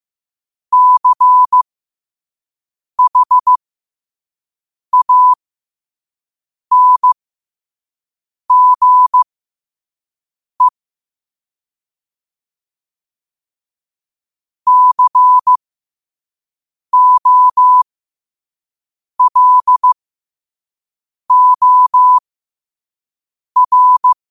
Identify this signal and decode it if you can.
Morse code, heard as a keyed tone, about 15 wpm, spaced as for 5 wpm, 1000 Hz. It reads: CHANGE COLOR
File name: qotdfiles / 5wpm